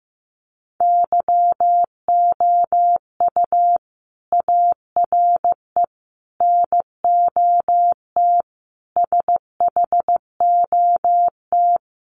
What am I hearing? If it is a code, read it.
Morse code, heard as a keyed tone, about 15 wpm, 700 Hz. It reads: YOU ARE NOT SHOT
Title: Sentences from Top 1100 Words 15wpm